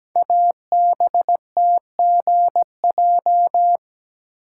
Morse code, heard as a keyed tone, about 17 wpm, 700 Hz. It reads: ABTGJ
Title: Code Groups - 5 Letters - 17wpm